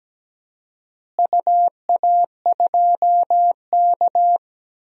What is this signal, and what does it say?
Morse code, heard as a keyed tone, about 17 wpm, 700 Hz. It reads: UA2K
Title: DXCC-100 Call Signs - 4 Characters - Repeated 3x - 17wpm